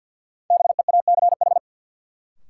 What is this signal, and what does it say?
Morse code, heard as a keyed tone, about 53 wpm, 700 Hz. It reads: BEACH
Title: Sentences from Top 1500 Words - New Word Drill - 50wpm